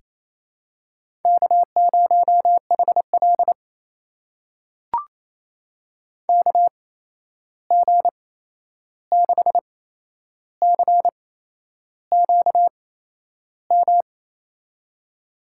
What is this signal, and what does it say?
Morse code, heard as a keyed tone, about 28 wpm, spaced as for 11 wpm, 700 Hz. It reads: K0HL KG6CQM
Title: USA Call Signs - 8x Character Spacing - 28wpm